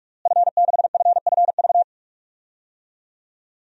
Morse code, heard as a keyed tone, about 43 wpm, 700 Hz. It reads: F6FFV
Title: DXCC-100 Call Signs - 5 Characters - Repeated 2x - 45wpm